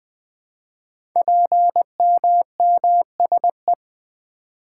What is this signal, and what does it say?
Morse code, heard as a keyed tone, about 20 wpm, 700 Hz. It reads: PMMSE
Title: Code Groups - 5 Letters - 20wpm